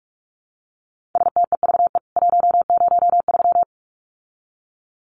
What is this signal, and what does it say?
Morse code, heard as a keyed tone, about 44 wpm, 700 Hz. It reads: STEVE 103
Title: CWT Exchange 45wpm